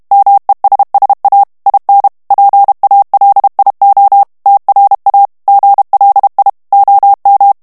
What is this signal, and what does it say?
Morse code, heard as a keyed tone, about 32 wpm, 800 Hz. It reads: MESSA IN PALIO TRA GLI OM
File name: texte clair 2 vitesse 30 mots minute